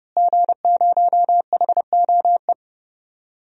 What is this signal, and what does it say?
Morse code, heard as a keyed tone, about 32 wpm, 700 Hz. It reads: G0HOE
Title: DXCC-100 Call Signs - 5 Characters - Repeated 3x - 30wpm